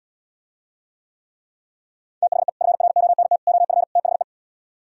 Morse code, heard as a keyed tone, about 77 wpm, 700 Hz. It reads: THE FRONT OF THE